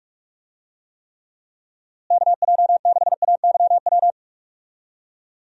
Morse code, heard as a keyed tone, about 42 wpm, 700 Hz. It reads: KJ6AYW
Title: USA Call Signs - 6 Characters - Repeated 3x - 45wpm